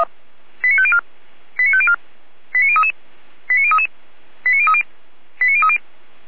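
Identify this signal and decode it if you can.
Five-tone selective call (ZVEI), heard as quick runs of five tones, tones of about 80 ms. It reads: 89504 89504 89033 89033 89030 89030
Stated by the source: ZVEI five-tone selective call received with an SDR.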